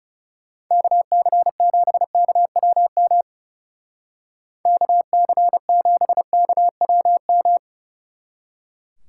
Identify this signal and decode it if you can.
Morse code, heard as a keyed tone, about 36 wpm, 700 Hz. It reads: KC7KWM KC7KWM
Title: Speed Racing - USA 2x3 Call Signs 30wpm